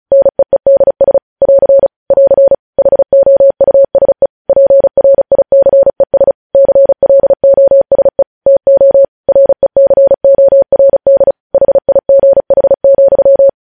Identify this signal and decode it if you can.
Morse code, heard as a keyed tone, about 36 wpm, 550 Hz. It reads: NEEDS <AR> <AR> HOUSE PRICES CLOSE TO RECORD HIGH,